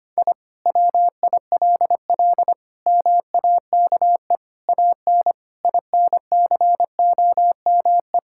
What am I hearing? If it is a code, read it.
Morse code, heard as a keyed tone, about 25 wpm, 700 Hz. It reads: I WILL MAKE AN INCOME
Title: Sentences from Top 1000 Words - Review Part 5 - 25wpm